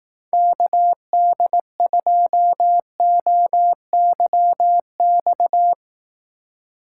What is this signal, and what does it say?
Morse code, heard as a keyed tone, about 18 wpm, 700 Hz. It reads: KD2OYX
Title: Speed Racing - USA Call Signs 15wpm